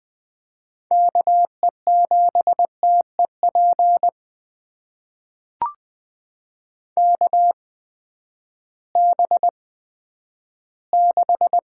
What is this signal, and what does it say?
Morse code, heard as a keyed tone, about 20 wpm, spaced as for 11 wpm, 700 Hz. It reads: KE7TEP KB6
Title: USA Call Signs - 6 Characters - 8x Character Spacing - 20wpm